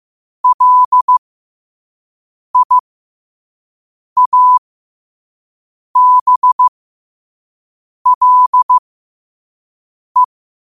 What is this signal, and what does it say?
Morse code, heard as a keyed tone, about 15 wpm, spaced as for 5 wpm, 1000 Hz. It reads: LIABLE
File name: qotdfiles / 5wpm